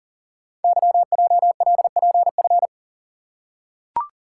Morse code, heard as a keyed tone, about 40 wpm, 700 Hz. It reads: YJLPF
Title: Code Groups - 5 Letters - 40wpm